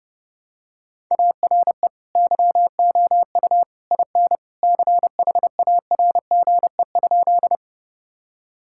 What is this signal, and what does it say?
Morse code, heard as a keyed tone, about 30 wpm, 700 Hz. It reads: ARE YOU IN CHARGE?
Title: Sentences from Top 1000 Words - Review Part 6 - 30wpm